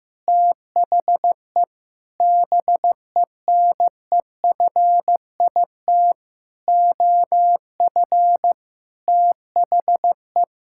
Morse code, heard as a keyed tone, about 15 wpm, 700 Hz. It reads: THE BENEFIT OF THE
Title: Sentences from Top 1000 Words - Review Part 6 - 15wpm